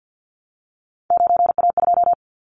Morse code, heard as a keyed tone, about 50 wpm, 700 Hz. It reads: 9A2
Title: Callsign Prefixes for Top 50 DXCC - 50wpm